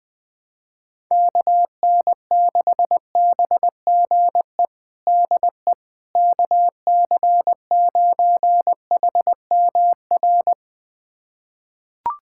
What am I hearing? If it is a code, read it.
Morse code, heard as a keyed tone, about 20 wpm, 700 Hz. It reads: KN6BGE DE KC9HMR
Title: USA Callsign DE Callsign 20wpm